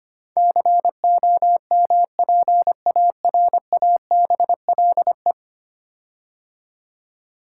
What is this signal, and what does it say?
Morse code, heard as a keyed tone, about 25 wpm, 700 Hz. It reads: COMPARABLE
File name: 10 Letter Words - First 3 Letters Given - 25wpm